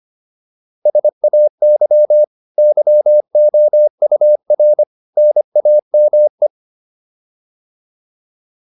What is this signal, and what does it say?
Morse code, heard as a keyed tone, about 25 wpm, 600 Hz. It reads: SAY YOUR NAME